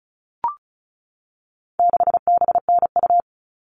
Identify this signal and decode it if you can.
Morse code, heard as a keyed tone, about 34 wpm, 700 Hz. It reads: 6BNU
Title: All Cut Numbers - 4-Digit 35wpm